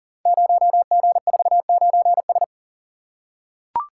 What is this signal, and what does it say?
Morse code, heard as a keyed tone, about 39 wpm, 700 Hz. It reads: 0G49S
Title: Code Groups - 5 Letter:Numbers - 40wpm